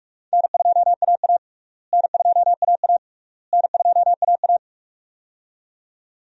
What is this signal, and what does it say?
Morse code, heard as a keyed tone, about 46 wpm, 700 Hz. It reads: N2AA N2AA N2AA
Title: USA Call Signs - 4 Characters - Repeated 3x - 45wpm